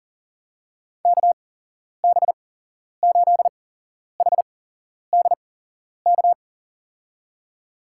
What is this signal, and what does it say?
Morse code, heard as a keyed tone, about 37 wpm, spaced as for 11 wpm, 700 Hz. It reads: KB8HDK
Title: USA Call Signs - 8x Character Spacing - 40wpm